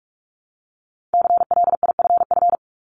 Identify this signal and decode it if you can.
Morse code, heard as a keyed tone, about 46 wpm, 700 Hz. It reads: CLIFF
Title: Multi-letter Phonemes Repeated 3x 45wpm